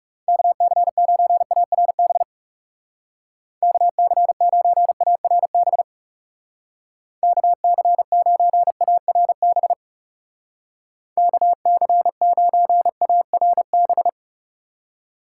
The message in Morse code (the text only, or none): KC9ARB KC9ARB KC9ARB KC9ARB